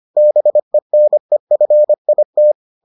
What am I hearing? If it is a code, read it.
Morse code, heard as a keyed tone, about 25 wpm, 600 Hz. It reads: BENEFIT